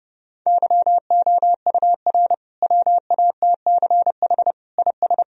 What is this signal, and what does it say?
Morse code, heard as a keyed tone, about 32 wpm, 700 Hz. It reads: YOUR WATCH IS
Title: Sentences from Top 1000 Words - Review Part 2 - 30wpm